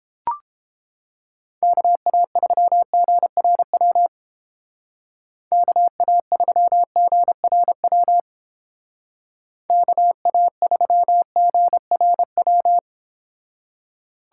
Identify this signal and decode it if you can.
Morse code, heard as a keyed tone, about 29 wpm, 700 Hz. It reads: KA3GRW KA3GRW KA3GRW